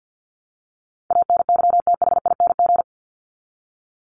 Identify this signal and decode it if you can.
Morse code, heard as a keyed tone, about 48 wpm, 700 Hz. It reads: ANYTHING
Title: Top 751-1000 Words 50wpm